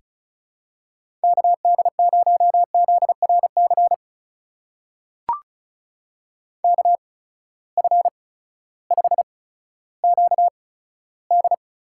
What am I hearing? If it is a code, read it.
Morse code, heard as a keyed tone, about 36 wpm, spaced as for 16 wpm, 700 Hz. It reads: KD0ZRC KF5QD
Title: USA Call Signs - 6 Characters - 8x Character Spacing - 35wpm